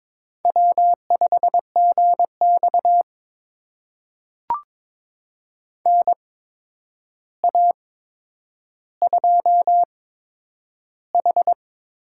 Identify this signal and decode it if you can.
Morse code, heard as a keyed tone, about 22 wpm, spaced as for 9 wpm, 700 Hz. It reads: W5GX NA2H
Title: USA Call Signs - 4 Characters - 8x Character Spacing - 22wpm